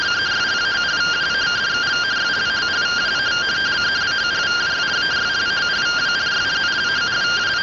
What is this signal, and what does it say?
Two-tone teleprinter signal, about 170 Hz shift.